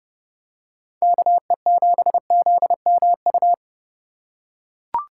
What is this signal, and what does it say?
Morse code, heard as a keyed tone, about 29 wpm, 700 Hz. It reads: KE7ZMU
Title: USA Call Signs - 6 Characters - 4x Character Spacing - 30wpm